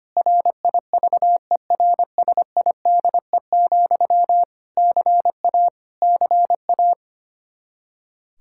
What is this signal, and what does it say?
Morse code, heard as a keyed tone, about 25 wpm, 700 Hz. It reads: RIVERSIDE, CA CA
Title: K1USN SST QSO Practice 25wpm